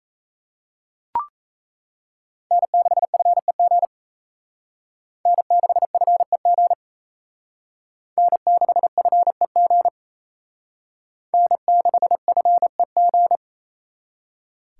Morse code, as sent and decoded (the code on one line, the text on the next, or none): -. -.... ..-. --. / -. -.... ..-. --. / -. -.... ..-. --. / -. -.... ..-. --.
N6FEG N6FEG N6FEG N6FEG